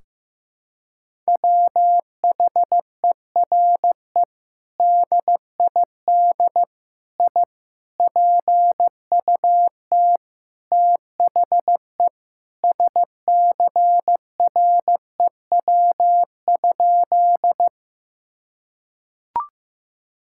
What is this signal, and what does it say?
Morse code, heard as a keyed tone, about 16 wpm, 700 Hz. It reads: WHERE DID I PUT THE SCREW?
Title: Sentences from Top 1600 Words 15wpm